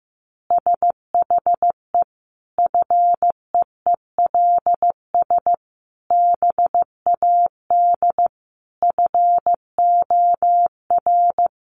Morse code, heard as a keyed tone, about 15 wpm, 700 Hz. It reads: SHE FEELS BAD FOR
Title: Additional Easy Phrases 15wpm